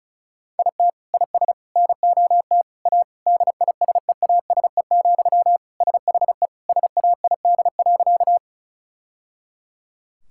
Morse code, heard as a keyed tone, about 34 wpm, 700 Hz. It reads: IT IS NOT A DISEASE, SHE SAID.
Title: Sentences from Top 1000 Words - Review Part 6 - 35wpm